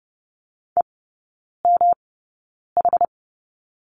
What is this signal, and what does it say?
Morse code, heard as a keyed tone, about 30 wpm, spaced as for 9 wpm, 700 Hz. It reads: EMH
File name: Lesson 18e - Letter m - Three Characters 30wpm